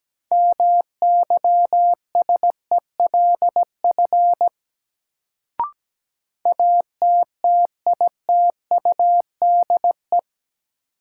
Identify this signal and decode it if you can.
Morse code, heard as a keyed tone, about 17 wpm, 700 Hz. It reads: MYSELF ATTITUDE
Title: Sentences from Top 1300 Words - New Word Drill Alt - 17wpm